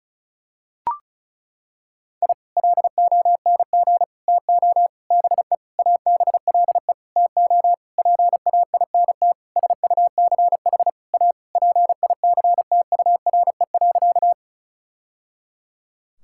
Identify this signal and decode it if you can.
Morse code, heard as a keyed tone, about 34 wpm, 700 Hz. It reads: I LONG TO BE ABLE TO PAINT SUCH A PICTURE.
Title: Sentences from Top 1000 Words - Review Part 6 - 35wpm